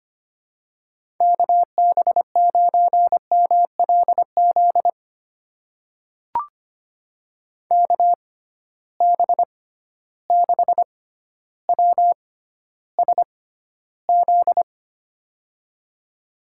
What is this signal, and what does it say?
Morse code, heard as a keyed tone, about 25 wpm, spaced as for 13 wpm, 700 Hz. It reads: KB9MLZ KB6WSZ
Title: USA Call Signs - 6 Characters - 6x Character Spacing - 25wpm